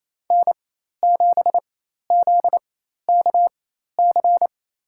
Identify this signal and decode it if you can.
Morse code, heard as a keyed tone, about 28 wpm, spaced as for 13 wpm, 700 Hz. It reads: N7ZKC
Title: USA Call Signs - 4x Character Spacing - 28wpm